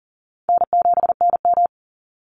Morse code, heard as a keyed tone, about 40 wpm, 700 Hz. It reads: N7NM
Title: Call Signs Repeated 2x - 40wpm